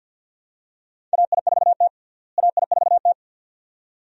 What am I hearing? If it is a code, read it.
Morse code, heard as a keyed tone, about 46 wpm, 700 Hz. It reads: AI4T AI4T
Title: USA Call Signs - 4 Characters - Repeated 2x - 50wpm